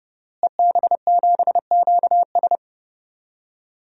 Morse code, heard as a keyed tone, about 30 wpm, 700 Hz. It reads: EB7QS
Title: DXCC-100 Call Signs - 5 Characters - Repeated 3x - 30wpm